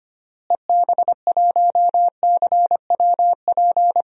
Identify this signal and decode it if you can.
Morse code, heard as a keyed tone, about 25 wpm, 700 Hz. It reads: EB1CWP
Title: DXCC-100 Call Signs - 6 Characters - Repeated 3x - 25wpm